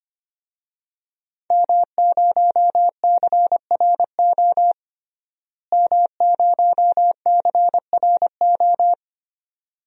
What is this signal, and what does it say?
Morse code, heard as a keyed tone, about 25 wpm, 700 Hz. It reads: M0CRO M0CRO
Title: DXCC-100 Call Signs - 5 Characters - Repeated 2x - 25wpm